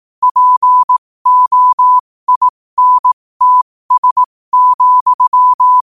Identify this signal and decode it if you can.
Morse code, heard as a keyed tone, about 18 wpm, spaced as for 15 wpm, 1000 Hz. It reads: POINTS,